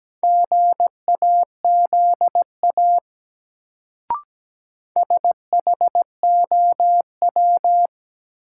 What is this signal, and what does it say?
Morse code, heard as a keyed tone, about 17 wpm, 700 Hz. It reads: GAZA SHOW
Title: Sets of 5 Words - New Word Drill - 17wpm